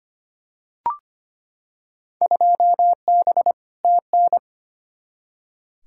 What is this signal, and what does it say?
Morse code, heard as a keyed tone, about 25 wpm, 700 Hz. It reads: 2B TN
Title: Field Day Exchanges Encore - 25wpm